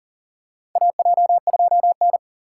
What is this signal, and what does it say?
Morse code, heard as a keyed tone, about 40 wpm, 700 Hz. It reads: AJ2N
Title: USA Call Signs - 4 Characters - 3x Character Spacing - 40wpm